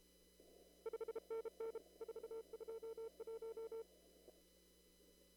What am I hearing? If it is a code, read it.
Morse code, heard as a keyed tone, about 34 wpm, 450 Hz. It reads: 5NN 421
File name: CW / kt5j wpx cw-op-k5pi